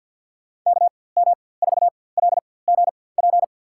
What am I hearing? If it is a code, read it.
Morse code, heard as a keyed tone, about 52 wpm, spaced as for 23 wpm, 700 Hz. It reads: KM4LGP
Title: USA Call Signs - 4x Character Spacing - 50wpm